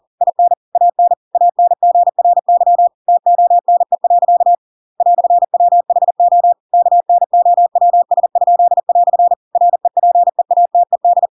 Morse code, heard as a keyed tone, about 40 wpm, 700 Hz. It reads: IN AN ANGRY TONE. "WHO KNOWS?" REPEATED
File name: Pinocchio / Pinocchio ch7 40 40 ik1hge